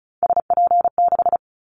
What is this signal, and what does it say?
Morse code, heard as a keyed tone, about 34 wpm, 700 Hz. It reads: SP6
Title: Callsign Prefixes for Top 25 DXCC - 35wpm